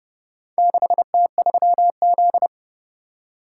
Morse code, heard as a keyed tone, about 28 wpm, 700 Hz. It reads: 6T3Z